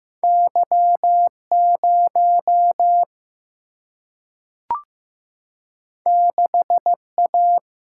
Morse code, heard as a keyed tone, about 15 wpm, 700 Hz. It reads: Y0 6A
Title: Single-Letter Single-Number 15wpm